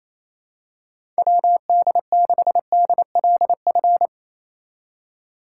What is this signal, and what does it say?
Morse code, heard as a keyed tone, about 27 wpm, 700 Hz. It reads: WD6DLF
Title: USA Call Signs - 6x Character Spacing - 28wpm